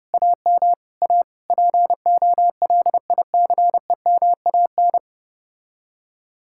Morse code, heard as a keyed tone, about 32 wpm, 700 Hz. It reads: AM A POLICEMAN